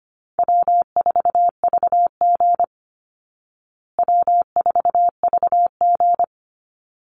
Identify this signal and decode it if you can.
Morse code, heard as a keyed tone, about 25 wpm, 700 Hz. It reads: W4VG W4VG
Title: Call Signs Repeated 3x - 25wpm